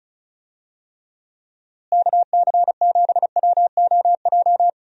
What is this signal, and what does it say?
Morse code, heard as a keyed tone, about 34 wpm, 700 Hz. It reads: KC7WOJ
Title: USA Call Signs - 6 Characters - Repeated 3x - 35wpm